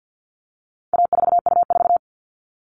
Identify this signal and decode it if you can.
Morse code, heard as a keyed tone, about 50 wpm, 700 Hz. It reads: A4U4
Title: All Cut Numbers - 4-Digit 50wpm